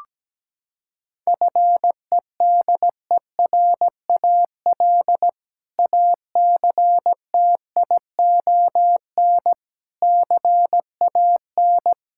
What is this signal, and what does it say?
Morse code, heard as a keyed tone, about 17 wpm, 700 Hz. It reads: FEDERAL ACTION CAN
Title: Sentences from Top 1000 Words - Review Part 5 - 17wpm